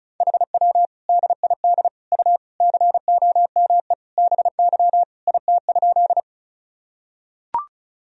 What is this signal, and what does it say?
Morse code, heard as a keyed tone, about 34 wpm, 700 Hz. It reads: HW DID U COME BY IT?
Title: Sentences from Top 100 Words and Abbreviations 35wpm